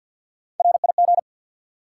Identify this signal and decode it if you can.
Morse code, heard as a keyed tone, about 53 wpm, 700 Hz. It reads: RIG
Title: Top 90 QSO Elements 50wpm